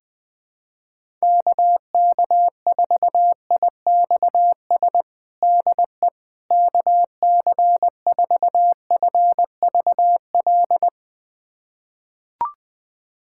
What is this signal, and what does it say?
Morse code, heard as a keyed tone, about 20 wpm, 700 Hz. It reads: KK4IXS DE KC4FVL
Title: USA Callsign DE Callsign - 6 Characters - 20wpm